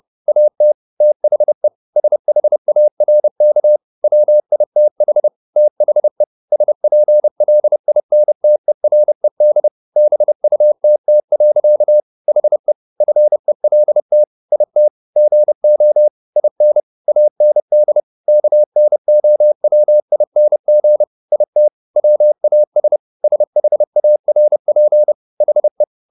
Morse code, heard as a keyed tone, about 30 wpm, 600 Hz. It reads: AT THE SHARK WITH THE SPLINTERED BUTT. HE FELT IT GO IN AND KNOWING IT WAS SHARP HE